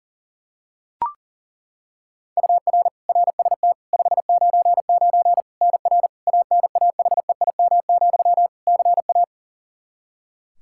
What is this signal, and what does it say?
Morse code, heard as a keyed tone, about 37 wpm, 700 Hz. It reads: UR RST 599 NR ANAHEIM, CA
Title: Practice QSO - Exchange 1 - Beginning - 40wpm